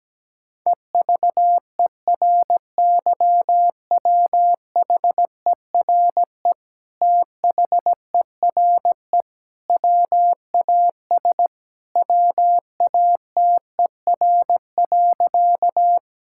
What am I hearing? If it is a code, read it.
Morse code, heard as a keyed tone, about 17 wpm, 700 Hz. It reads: EVERYWHERE THERE WAS WATER.
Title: Sentences from Top 1600 Words 17wpm